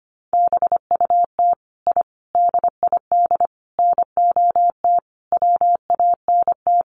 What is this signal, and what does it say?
Morse code, heard as a keyed tone, about 25 wpm, 700 Hz. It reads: BUT I DID NOT WANT
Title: Sets of 5 Words Encore 25wpm